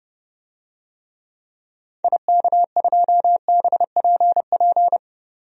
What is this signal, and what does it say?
Morse code, heard as a keyed tone, about 30 wpm, 700 Hz. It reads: IK2BPP
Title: DXCC-100 Call Signs - 6 Characters - 30wpm